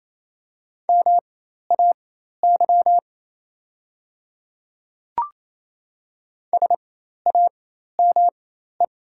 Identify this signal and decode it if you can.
Morse code, heard as a keyed tone, about 28 wpm, spaced as for 13 wpm, 700 Hz. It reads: MAY SAME
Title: 4x Character Spacing - Top 300 Words - 28wpm